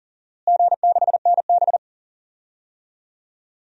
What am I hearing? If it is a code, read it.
Morse code, heard as a keyed tone, about 40 wpm, 700 Hz. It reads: G6NB
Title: DXCC-100 Call Signs - 4 Characters - Repeated 2x - 40wpm